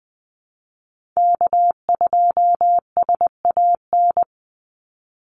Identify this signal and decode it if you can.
Morse code, heard as a keyed tone, about 20 wpm, 700 Hz. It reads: K2SAN
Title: Call Signs - 20wpm